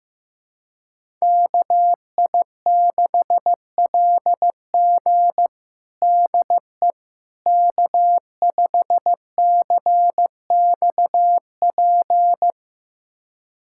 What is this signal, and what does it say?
Morse code, heard as a keyed tone, about 15 wpm, 700 Hz. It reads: KI6LG DE K5CXP
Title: USA Callsign DE Callsign - 5 Characters - 15wpm